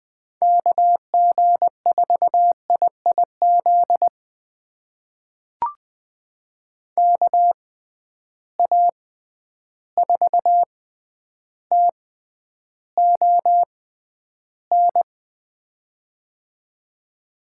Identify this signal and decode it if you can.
Morse code, heard as a keyed tone, about 20 wpm, spaced as for 10 wpm, 700 Hz. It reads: KG4IIZ KA4TON
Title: USA Call Signs - 6 Characters - 6x Character Spacing - 20wpm